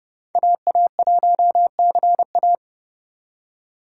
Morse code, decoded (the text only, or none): AA1CA